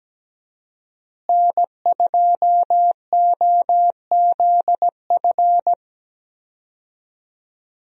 Morse code, heard as a keyed tone, about 17 wpm, 700 Hz. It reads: N2OZF
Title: USA Call Signs - 5 Characters - Repeated 2x - 17wpm